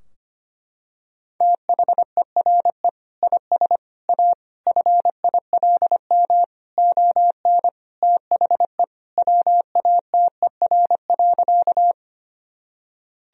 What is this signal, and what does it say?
Morse code, heard as a keyed tone, about 25 wpm, 700 Hz. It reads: THERE IS A FILM ON THE WATER.
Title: Sentences from Top 1000 Words - Review Part 3 - 25wpm